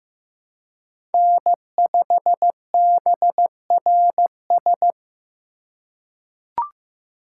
Morse code, heard as a keyed tone, about 15 wpm, 700 Hz. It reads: N5BRS
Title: USA Call Signs - 4x Character Spacing - 15wpm